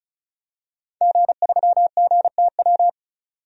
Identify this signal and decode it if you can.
Morse code, heard as a keyed tone, about 34 wpm, 700 Hz. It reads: G3GTW
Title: DXCC-100 Call Signs - 5 Characters - 35wpm